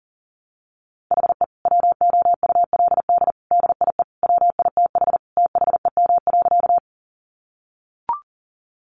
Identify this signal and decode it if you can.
Morse code, heard as a keyed tone, about 38 wpm, 700 Hz. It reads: HE WOULD DIE WITH THEM.
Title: Sentences from Top 1300 Words 40wpm